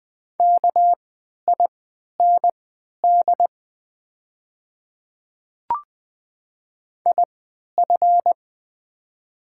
Morse code, heard as a keyed tone, about 20 wpm, spaced as for 11 wpm, 700 Hz. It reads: KIND IF
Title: 3x Character Spacing - Top 300 Words - 20wpm